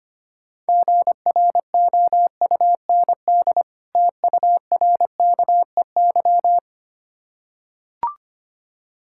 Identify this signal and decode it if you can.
Morse code, heard as a keyed tone, about 25 wpm, 700 Hz. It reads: GROUND TURKEY
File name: practice / 25wpm / Difficult Words with Hints - 25wpm